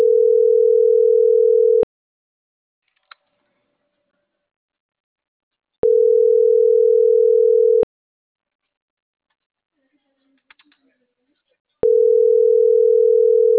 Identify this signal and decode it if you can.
Ringback tone (North America), 440 + 480 Hz.